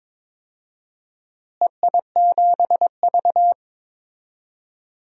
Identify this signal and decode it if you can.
Morse code, heard as a keyed tone, about 22 wpm, 700 Hz. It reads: EI7V
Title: DXCC-100 Call Signs - 4 Characters - Repeated 3x - 22wpm